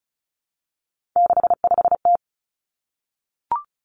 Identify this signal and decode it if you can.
Morse code, heard as a keyed tone, about 36 wpm, 700 Hz. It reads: 65T
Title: Three Challenging Letter-Number Combinations 35wpm